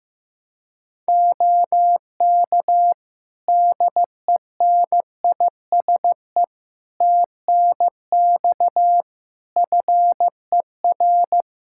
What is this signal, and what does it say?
Morse code, heard as a keyed tone, about 15 wpm, 700 Hz. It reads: OK DENISE TNX FER
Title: Practice QSO - Exchange 4 - Ending - 15wpm